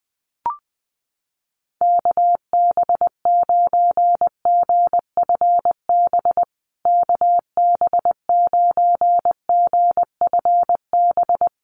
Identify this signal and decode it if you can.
Morse code, heard as a keyed tone, about 20 wpm, 700 Hz. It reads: KB9GFB KB9GFB
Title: USA Call Signs Repeated 2x 20wpm